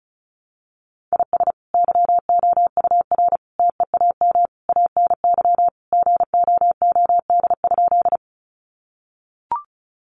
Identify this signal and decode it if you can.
Morse code, heard as a keyed tone, about 34 wpm, 700 Hz. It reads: IS YOUR TEAM ANY GOOD?
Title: Sentences from Top 1500 Words 35wpm